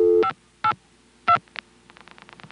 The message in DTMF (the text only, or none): 582